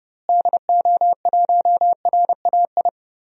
Morse code, heard as a keyed tone, about 30 wpm, 700 Hz. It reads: DO1RAI
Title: DXCC-100 Call Signs - 6 Characters - Repeated 2x - 30wpm